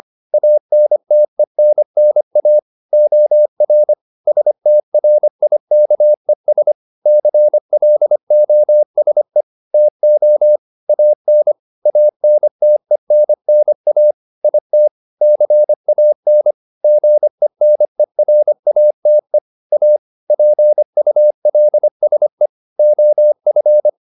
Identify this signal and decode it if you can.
Morse code, heard as a keyed tone, about 25 wpm, 600 Hz. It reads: ANTENNA OR STRIKES CLOSE TO AN ANTENNA IT CAN GENERATE A PULSE OF